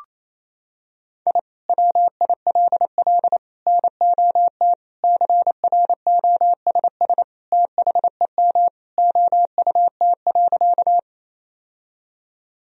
Morse code, heard as a keyed tone, about 28 wpm, 700 Hz. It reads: I WILL NOT CROSS THEM OUT.
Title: Sentences from Top 1000 Words - Review Part 1 - 28wpm